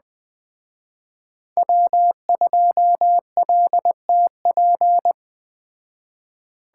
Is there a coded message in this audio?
Yes: Morse code from 1.6 to 5.1 s, heard as a keyed tone, about 21 wpm, 700 Hz. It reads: W2LTP